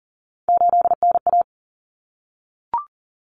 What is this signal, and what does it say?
Morse code, heard as a keyed tone, about 40 wpm, 700 Hz. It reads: 8NA
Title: Common Cut Numbers - 3-Digit 40wpm